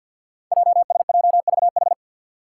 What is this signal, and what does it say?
Morse code, heard as a keyed tone, about 49 wpm, 700 Hz. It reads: JSJVH
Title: Five Challenging Letter-Number Combinations 50wpm